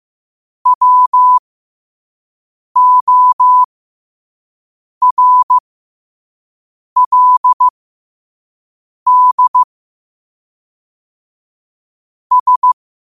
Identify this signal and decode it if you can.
Morse code, heard as a keyed tone, about 15 wpm, spaced as for 5 wpm, 1000 Hz. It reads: WORLD S
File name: qotdfiles / 5wpm